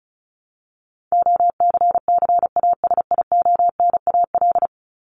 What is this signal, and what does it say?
Morse code, heard as a keyed tone, about 34 wpm, 700 Hz. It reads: OCCASIONAL
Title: LY Suffix - Top 100 Words with Base Words 35wpm